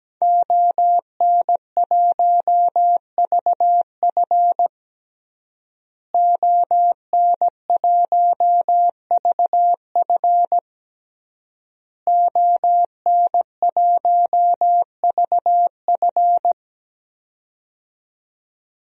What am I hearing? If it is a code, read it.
Morse code, heard as a keyed tone, about 17 wpm, 700 Hz. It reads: ON1VF ON1VF ON1VF
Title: DXCC-100 Call Signs - 5 Characters - Repeated 3x - 17wpm